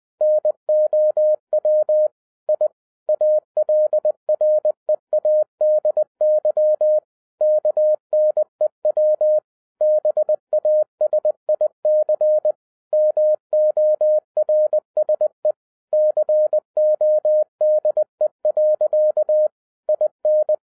Morse code, heard as a keyed tone, about 20 wpm, 610 Hz. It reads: NOW I ALREADY KNEW BASIC MORSE CODE. IN
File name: 2011-07-mp3 / n08 A LOT HAS CHANGED 20wpm